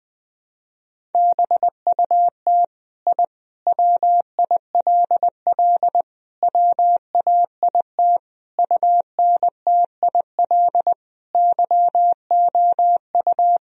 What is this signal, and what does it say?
Morse code, heard as a keyed tone, about 20 wpm, 700 Hz. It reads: BUT I WILL WAIT UNTIL YOU
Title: Sentences from Top 1000 Words - Review Part 3 - 20wpm